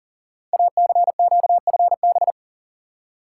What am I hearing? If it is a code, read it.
Morse code, heard as a keyed tone, about 44 wpm, 700 Hz. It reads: ACQFB
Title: Code Groups - 5 Letters - 40wpm